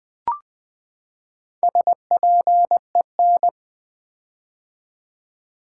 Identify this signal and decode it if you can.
Morse code, heard as a keyed tone, about 20 wpm, 700 Hz. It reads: SPEN
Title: Build A Word - 4 Letters At A Time - 8-Letter Words - 20wpm